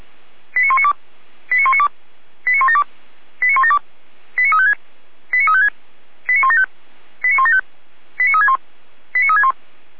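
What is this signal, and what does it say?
Five-tone selective call (ZVEI), heard as quick runs of five tones, tones of about 70 ms. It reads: 89192 89192 89172 89172 89356 89356 89175 89175 89361 89361
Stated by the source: ZVEI five-tone selective call received by SDR, Feuerwehr Orlamünde.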